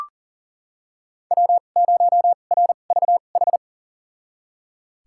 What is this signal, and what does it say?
Morse code, heard as a keyed tone, about 40 wpm, spaced as for 29 wpm, 700 Hz. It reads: W0RVH
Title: USA Call Signs - 5 Characters - 2x Character Spacing - 40wpm